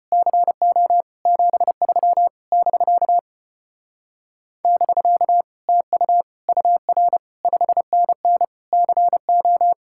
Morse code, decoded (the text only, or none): CO 73 <BK> <BK> TU UR 5NN CO